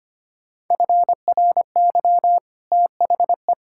Morse code, heard as a keyed tone, about 25 wpm, 690 Hz. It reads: FRY THE
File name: Easy Sentences with Hints - 25wpm